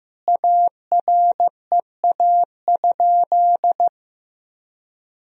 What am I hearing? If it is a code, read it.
Morse code, heard as a keyed tone, about 15 wpm, 700 Hz. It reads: AREA?
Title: Sentences from Top 1000 Words - Review Part 2 - 15wpm